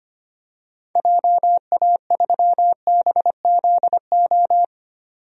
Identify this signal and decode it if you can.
Morse code, heard as a keyed tone, about 25 wpm, 690 Hz. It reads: JA3BZO
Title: DXCC-100 Call Signs - 6 Characters - 25wpm